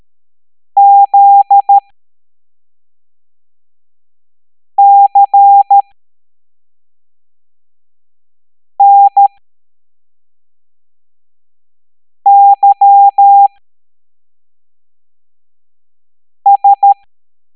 Morse code, heard as a keyed tone, about 13 wpm, spaced as for 3 wpm, 800 Hz. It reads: ZCNYS
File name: hamdata / L07-EX02-HQP